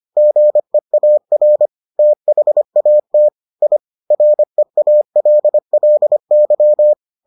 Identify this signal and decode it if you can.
Morse code, heard as a keyed tone, about 25 wpm, 600 Hz. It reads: GEAR THAT I REALLY